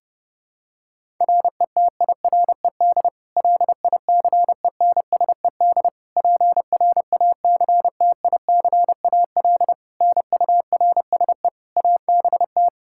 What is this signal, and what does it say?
Morse code, heard as a keyed tone, about 32 wpm, 700 Hz. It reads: RETIRED LICENSED PRACTICAL NURSE ABT
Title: K1USN SST QSO Practice 30wpm